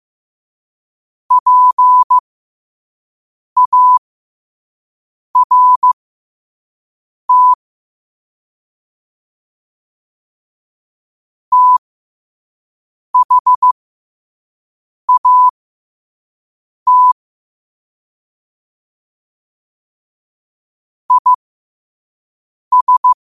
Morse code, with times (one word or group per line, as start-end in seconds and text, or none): 1.3-7.5 PART
11.5-17.1 THAT
21.1-23.1 IS